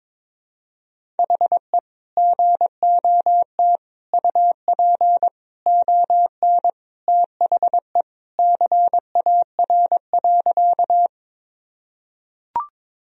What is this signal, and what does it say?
Morse code, heard as a keyed tone, about 22 wpm, 700 Hz. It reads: HE GOT UP ON THE CAR.
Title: Sentences from Top 1000 Words - Review Part 1 - 22wpm